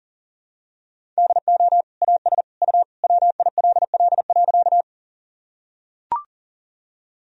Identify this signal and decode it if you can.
Morse code, heard as a keyed tone, about 40 wpm, 700 Hz. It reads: DO AS U WILL.
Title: Sentences from Top 100 Words and Abbreviations 40wpm